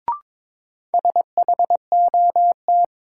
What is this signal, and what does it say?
Morse code, heard as a keyed tone, about 22 wpm, 700 Hz. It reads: SHOT
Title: Sentences from Top 1100 Words - New Word Drill - 22wpm